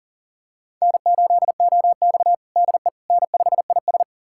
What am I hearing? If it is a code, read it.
Morse code, heard as a keyed tone, about 40 wpm, 700 Hz. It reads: N8OX DE N5IS
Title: USA Callsign DE Callsign - 4 Characters - 40wpm